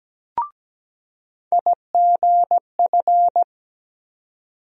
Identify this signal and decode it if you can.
Morse code, heard as a keyed tone, about 17 wpm, 700 Hz. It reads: IGF
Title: USA Call Sign Postfixes - 3 Letters 17wpm